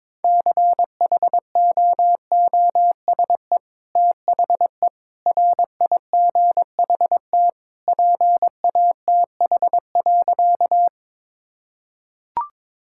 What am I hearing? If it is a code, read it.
Morse code, heard as a keyed tone, about 22 wpm, 700 Hz. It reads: CHOOSE THE RIGHT PATH.
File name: practice / 22wpm / Sentences with 2-Word Phrases - 22wpm